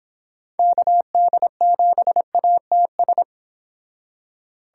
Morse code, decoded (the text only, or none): KD7ATS